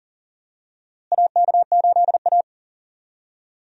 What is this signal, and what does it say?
Morse code, heard as a keyed tone, about 37 wpm, 700 Hz. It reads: AK8A
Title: USA Call Signs - 4 Characters - Repeated 3x - 40wpm